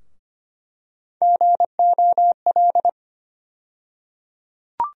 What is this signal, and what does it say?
Morse code, heard as a keyed tone, about 25 wpm, 700 Hz. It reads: GOL
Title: USA Call Sign Postfixes - 3 Letters 25wpm